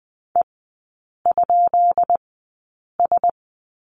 Morse code, heard as a keyed tone, about 20 wpm, spaced as for 8 wpm, 700 Hz. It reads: E?S
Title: 133 Lesson 22e - Question Mark - Three Characters 20wpm